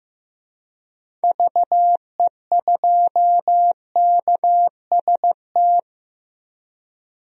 Morse code, heard as a keyed tone, about 15 wpm, 700 Hz. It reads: VE2KST
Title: DXCC-100 Call Signs - 6 Characters - 15wpm